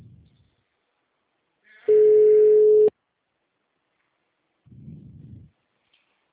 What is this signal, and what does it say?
Ringback tone (Europe, 425 Hz).
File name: out- -2037- - - .33101